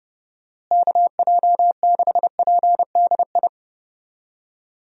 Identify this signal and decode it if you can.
Morse code, heard as a keyed tone, about 30 wpm, 700 Hz. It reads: KJ6PDI
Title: USA Call Signs - 6 Characters - Repeated 3x - 30wpm